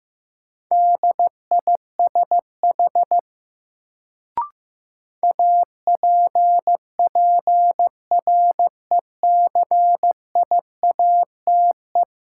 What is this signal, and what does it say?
Morse code, heard as a keyed tone, about 15 wpm, 700 Hz. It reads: DISH APPRECIATE
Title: Sentences from Top 1300 Words - New Word Drill Alt - 15wpm